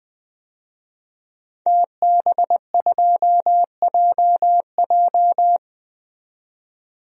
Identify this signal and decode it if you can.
Morse code, heard as a keyed tone, about 20 wpm, 700 Hz. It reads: TB2JJ
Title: DXCC-100 Call Signs - 5 Characters - 20wpm